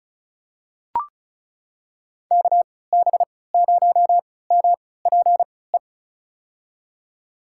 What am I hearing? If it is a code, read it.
Morse code, heard as a keyed tone, about 36 wpm, spaced as for 20 wpm, 700 Hz. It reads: KB0MPE
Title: USA Call Signs - 3x Character Spacing - 35wpm